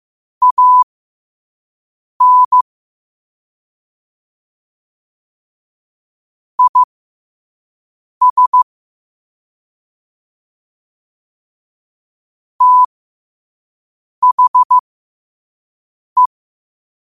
Morse code, heard as a keyed tone, about 15 wpm, spaced as for 5 wpm, 1000 Hz. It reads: AN IS THE
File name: 2024-10-07-5wpm